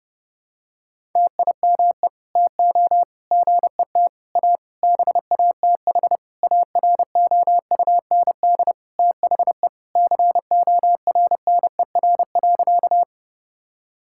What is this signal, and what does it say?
Morse code, heard as a keyed tone, about 30 wpm, 700 Hz. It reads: TIME TO GET A BATH AROUND THE CORNER.
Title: Sentences from Top 1600 Words 30wpm